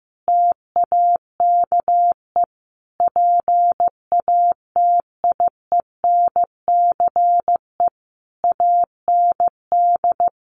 Morse code, heard as a keyed tone, about 15 wpm, 700 Hz. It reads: TAKE PATIENCE AND
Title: Sentences from Top 1300 Words 15wpm